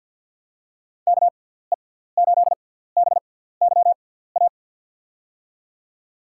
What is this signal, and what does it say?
Morse code, heard as a keyed tone, about 52 wpm, spaced as for 17 wpm, 700 Hz. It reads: KE8BYA
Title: USA Call Signs - 6 Characters - 6x Character Spacing - 50wpm